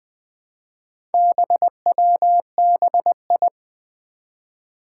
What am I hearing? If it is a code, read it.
Morse code, heard as a keyed tone, about 20 wpm, 700 Hz. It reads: BWBI
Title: Four Challenging Letter-Number Combinations 20wpm